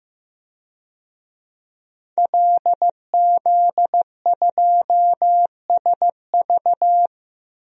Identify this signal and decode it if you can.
Morse code, heard as a keyed tone, about 15 wpm, 700 Hz. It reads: LZ2SV
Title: DXCC-100 Call Signs - 5 Characters - 15wpm